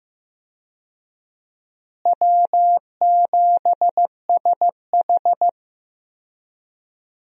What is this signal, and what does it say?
Morse code, heard as a keyed tone, about 15 wpm, 700 Hz. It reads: W7SH
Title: USA Call Signs - 4 Characters - Repeated 3x - 15wpm